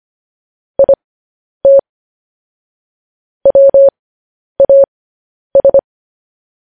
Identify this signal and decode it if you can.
Morse code, heard as a keyed tone, about 25 wpm, spaced as for 10 wpm, 550 Hz. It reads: IT WAS
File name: podcast / News headlines-1x-25x10